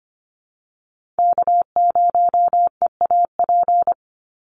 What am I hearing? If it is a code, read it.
Morse code, heard as a keyed tone, about 25 wpm, 700 Hz. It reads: K0EAP
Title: NA SOTA Chaser Callsigns 25wpm